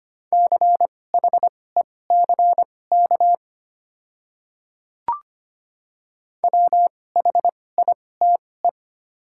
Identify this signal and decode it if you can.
Morse code, heard as a keyed tone, about 25 wpm, spaced as for 18 wpm, 700 Hz. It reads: CHECK WHITE
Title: 2x Character Spacing - Top 500 Words - 25wpm